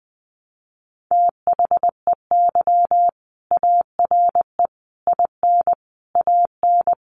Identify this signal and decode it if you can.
Morse code, heard as a keyed tone, about 19 wpm, 700 Hz. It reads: THEY ARE IN AN
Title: Sentences from Top 1000 Words 20wpm